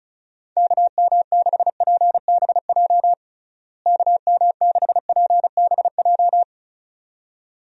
Morse code, heard as a keyed tone, about 36 wpm, 700 Hz. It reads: KM6PBJ KM6PBJ
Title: USA Call Signs - 6 Characters - Repeated 2x - 35wpm